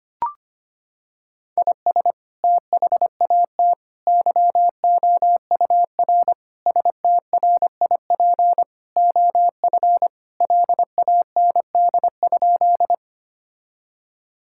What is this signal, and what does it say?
Morse code, heard as a keyed tone, about 25 wpm, 700 Hz. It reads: IS THAT YOUR STRIP OF LAND?
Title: Sentences from Top 1600 Words 25wpm